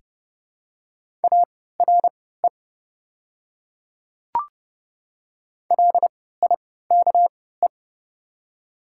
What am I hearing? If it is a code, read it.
Morse code, heard as a keyed tone, about 28 wpm, spaced as for 17 wpm, 690 Hz. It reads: ARE LIKE
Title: 3x Character Spacing - Top 100 Words - 30wpm